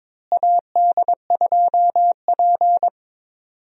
Morse code, heard as a keyed tone, about 22 wpm, 700 Hz. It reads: AD2P
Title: USA Call Signs - 4 Characters - Repeated 3x - 22wpm